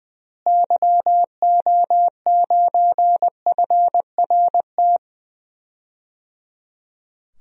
Morse code, heard as a keyed tone, about 21 wpm, 700 Hz. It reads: YO9FRT
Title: DXCC-100 Call Signs - 6 Characters - Repeated 3x - 20wpm